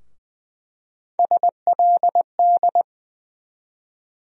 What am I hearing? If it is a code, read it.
Morse code, heard as a keyed tone, about 20 wpm, 700 Hz. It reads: SLD
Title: USA Call Sign Postfixes - 3 Letters 20wpm